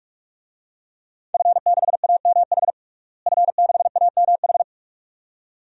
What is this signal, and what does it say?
Morse code, heard as a keyed tone, about 46 wpm, 700 Hz. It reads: F6AMH F6AMH
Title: DXCC-100 Call Signs - 5 Characters - Repeated 2x - 45wpm